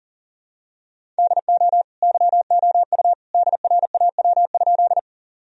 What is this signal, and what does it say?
Morse code, heard as a keyed tone, about 40 wpm, 700 Hz. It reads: DO YOU DRAW?
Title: Sentences from Top 1000 Words - Review Part 1 - 40wpm